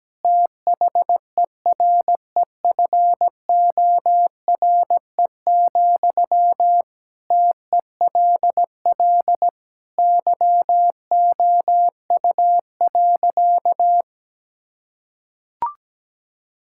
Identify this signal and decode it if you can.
Morse code, heard as a keyed tone, about 17 wpm, 700 Hz. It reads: THEREFORE, TELL YOU.
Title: Sentences from Top 1000 Words - Review Part 5 - 17wpm